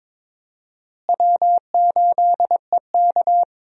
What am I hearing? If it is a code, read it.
Morse code, heard as a keyed tone, about 22 wpm, 700 Hz. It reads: W8EK
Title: USA Call Signs - 4 Characters - Repeated 2x - 22wpm